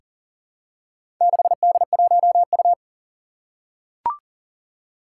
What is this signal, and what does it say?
Morse code, heard as a keyed tone, about 40 wpm, 700 Hz. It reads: 6D1U
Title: Four Challenging Letter-Number Combinations 40wpm